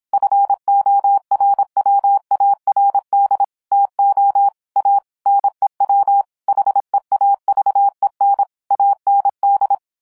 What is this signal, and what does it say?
Morse code, heard as a keyed tone, about 26 wpm, 800 Hz. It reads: FORWARD TO A NEW HEAVEN AND